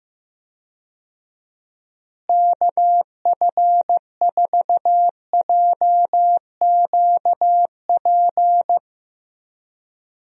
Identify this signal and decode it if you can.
Morse code, heard as a keyed tone, about 16 wpm, 700 Hz. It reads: KF4JQP
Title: USA Call Signs - 6 Characters - Repeated 2x - 15wpm